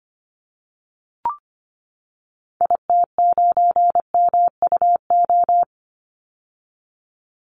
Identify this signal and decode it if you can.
Morse code, heard as a keyed tone, about 25 wpm, 700 Hz. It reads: IT9MUO
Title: Call Signs - 25wpm